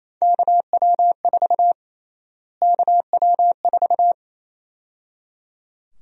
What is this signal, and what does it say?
Morse code, heard as a keyed tone, about 28 wpm, 700 Hz. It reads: KW4 KW4
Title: USA Call Sign Prefixes - Repeated 2x - 28wpm